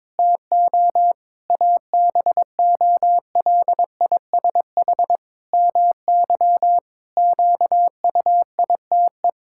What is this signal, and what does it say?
Morse code, heard as a keyed tone, about 22 wpm, 700 Hz. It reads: TO ABOLISH MY QUITE
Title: Pangram Challenge v2 - 22wpm